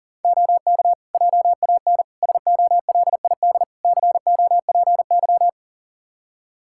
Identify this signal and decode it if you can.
Morse code, heard as a keyed tone, about 40 wpm, 700 Hz. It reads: OK JAN SOLID COPY
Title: K1USN SST QSO Practice 40wpm